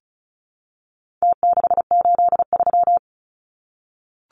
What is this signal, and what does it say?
Morse code, heard as a keyed tone, about 34 wpm, 700 Hz. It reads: T683
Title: Numbers - Contest Style - 4-Digit 35wpm